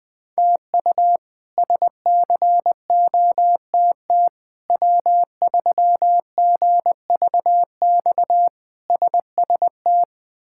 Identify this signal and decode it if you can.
Morse code, heard as a keyed tone, about 20 wpm, 700 Hz. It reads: TU SCOTT W3GVX SST
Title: K1USN SST QSO Practice - 20wpm